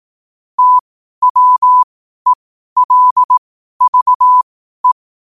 Morse code, heard as a keyed tone, about 18 wpm, spaced as for 12 wpm, 1000 Hz. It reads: TWELVE
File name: qotdfiles / 12wpm